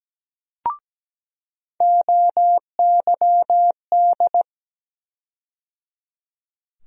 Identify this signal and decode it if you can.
Morse code, heard as a keyed tone, about 17 wpm, 700 Hz. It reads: OYD
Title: USA Call Sign Postfixes - 3 Letters 17wpm